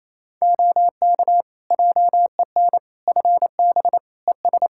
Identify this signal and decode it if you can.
Morse code, heard as a keyed tone, about 28 wpm, 700 Hz. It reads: OK JEN FB ES
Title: Practice QSO - Exchange 2 - Info - 28wpm